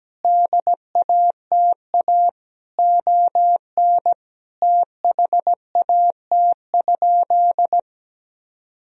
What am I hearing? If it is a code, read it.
Morse code, heard as a keyed tone, about 17 wpm, 700 Hz. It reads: DATA ON THAT?
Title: Sentences from Top 1000 Words - Review Part 3 - 17wpm